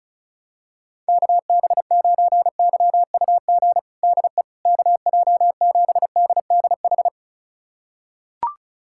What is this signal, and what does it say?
Morse code, heard as a keyed tone, about 36 wpm, 700 Hz. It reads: KB9YUG DE KJ7DDH
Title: USA Callsign DE Callsign - 6 Characters - 35wpm